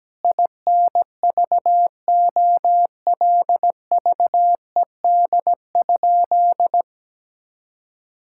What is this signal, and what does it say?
Morse code, heard as a keyed tone, about 17 wpm, 700 Hz. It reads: INVOLVED?
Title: Sentences from Top 1000 Words - Review Part 5 - 17wpm